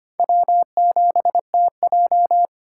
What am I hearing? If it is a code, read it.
Morse code, heard as a keyed tone, about 25 wpm, 700 Hz. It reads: W7TJ
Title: USA Call Signs - 4 Characters - 3x Character Spacing - 25wpm